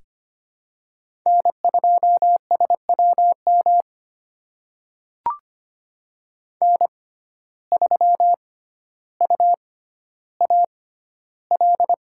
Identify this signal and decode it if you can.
Morse code, heard as a keyed tone, about 25 wpm, spaced as for 13 wpm, 700 Hz. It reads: N2SWM N3UAL
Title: USA Call Signs - 5 Characters - 6x Character Spacing - 25wpm